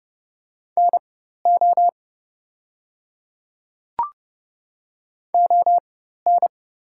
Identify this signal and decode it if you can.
Morse code, heard as a keyed tone, about 30 wpm, spaced as for 14 wpm, 700 Hz. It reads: NO ON
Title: 4x Character Spacing - Top 100 Words - 30wpm